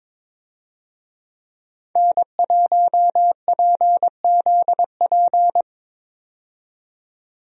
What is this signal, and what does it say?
Morse code, heard as a keyed tone, about 22 wpm, 700 Hz. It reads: N1PZP
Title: USA Call Signs - 5 Characters - 22wpm